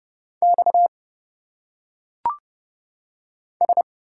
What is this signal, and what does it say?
Morse code, heard as a keyed tone, about 29 wpm, 700 Hz. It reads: X S
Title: Single Letters - Phonetic Pronunciation 30wpm